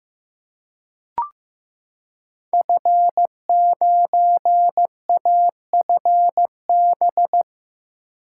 Morse code, heard as a keyed tone, about 15 wpm, 700 Hz. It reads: F9AFB
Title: Code Groups - 5 Letter:Numbers - 15wpm